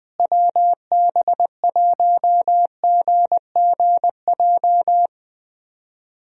Morse code, heard as a keyed tone, about 20 wpm, 700 Hz. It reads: WB1GGJ
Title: USA Call Signs - 6 Characters - Repeated 2x - 20wpm